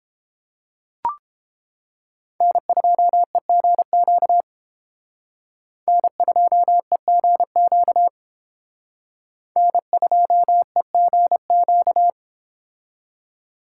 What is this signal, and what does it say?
Morse code, heard as a keyed tone, about 31 wpm, 700 Hz. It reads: N2EGQ N2EGQ N2EGQ